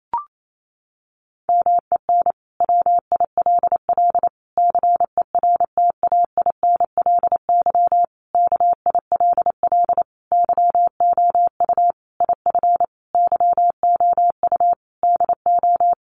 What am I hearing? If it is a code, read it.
Morse code, heard as a keyed tone, about 28 wpm, 700 Hz. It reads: MEN WILL CERTAINLY KILL YOU IF YOU DO
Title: Sentences from Top 1500 Words 28wpm